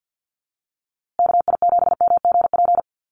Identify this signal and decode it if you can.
Morse code, heard as a keyed tone, about 49 wpm, 700 Hz. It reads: KI7MGP
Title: USA Call Signs Repeated 2x 50wpm